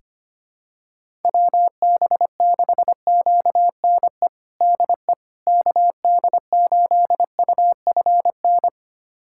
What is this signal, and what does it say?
Morse code, heard as a keyed tone, about 25 wpm, 700 Hz. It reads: WB6QNE DE KD8UFN
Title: USA Callsign DE Callsign - 6 Characters - 25wpm